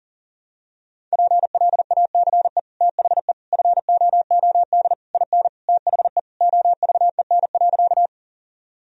Morse code, heard as a keyed tone, about 37 wpm, 700 Hz. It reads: PLACE THE FOOD IN THE OVEN.
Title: Sentences from Top 1000 Words - Review Part 4 - 40wpm